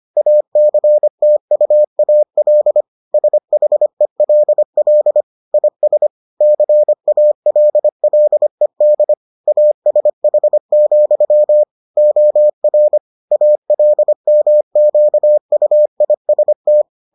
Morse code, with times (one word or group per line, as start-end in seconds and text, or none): 0.2-2.8 ACTUAL
3.1-5.2 SHELL
5.5-6.1 IS
6.4-9.1 CALLED
9.5-11.6 ASH,
12.0-13.0 OR
13.3-16.8 ALMQUIST